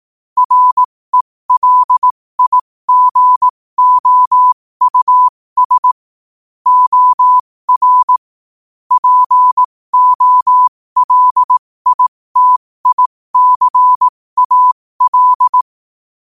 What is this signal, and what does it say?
Morse code, heard as a keyed tone, about 18 wpm, spaced as for 15 wpm, 1000 Hz. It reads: RELIGOUS OR POLITICAL
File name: qotdfiles / 15wpm